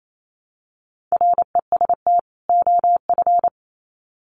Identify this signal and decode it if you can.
Morse code, heard as a keyed tone, about 28 wpm, 700 Hz. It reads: REST OF
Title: Sets of 2 Words 28wpm